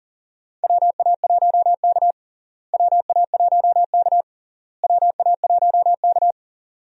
Morse code, heard as a keyed tone, about 40 wpm, 700 Hz. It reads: WA1K WA1K WA1K
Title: USA Call Signs - 4 Characters - Repeated 3x - 40wpm